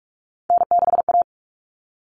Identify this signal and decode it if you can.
Morse code, heard as a keyed tone, about 44 wpm, 700 Hz. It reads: N6A
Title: All Cut Numbers - 3-Digit 45wpm